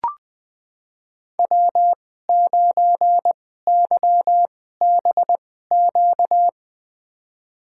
Morse code, heard as a keyed tone, about 21 wpm, spaced as for 15 wpm, 700 Hz. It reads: W9YBQ
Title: USA Call Signs - 5 Characters - 2x Character Spacing - 20wpm